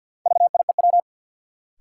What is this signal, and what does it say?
Morse code, heard as a keyed tone, about 53 wpm, 700 Hz. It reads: VIEW
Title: Sets of 5 Words - New Word Drill - 50wpm